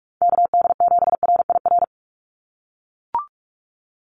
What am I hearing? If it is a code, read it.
Morse code, heard as a keyed tone, about 44 wpm, 700 Hz. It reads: KD7RIR
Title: USA Call Signs 45wpm